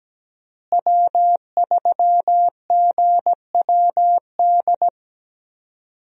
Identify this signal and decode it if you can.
Morse code, heard as a keyed tone, about 17 wpm, 700 Hz. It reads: W3GWD
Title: USA Call Signs - 5 Characters - Repeated 2x - 17wpm